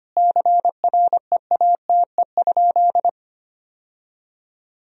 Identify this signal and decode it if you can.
Morse code, heard as a keyed tone, about 25 wpm, 700 Hz. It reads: CREATE?
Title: Sentences from Top 1000 Words - Review Part 3 - 25wpm